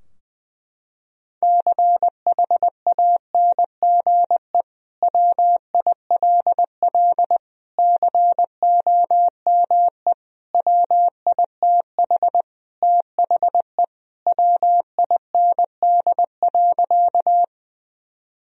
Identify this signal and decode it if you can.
Morse code, heard as a keyed tone, about 19 wpm, 700 Hz. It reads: CHANGE WILL COME WITH THE WIND.
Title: Sentences from Top 1000 Words - Review Part 2 - 20wpm